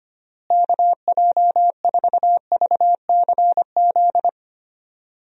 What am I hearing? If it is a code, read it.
Morse code, heard as a keyed tone, about 25 wpm, 700 Hz. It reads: KJ4VCZ
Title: USA Call Signs - 6 Characters - Repeated 2x - 25wpm